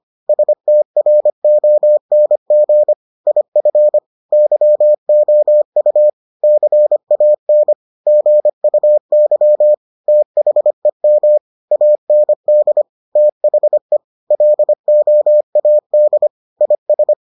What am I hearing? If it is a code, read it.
Morse code, heard as a keyed tone, about 25 wpm, 600 Hz. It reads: STRONG IF YOU CAN GUY THEM AND THE LOAD IS